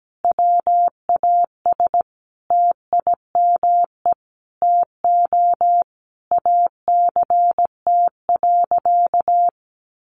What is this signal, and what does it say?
Morse code, heard as a keyed tone, about 17 wpm, 700 Hz. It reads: WAS TIME TO ACT.
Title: Sentences from Top 200 Words 17wpm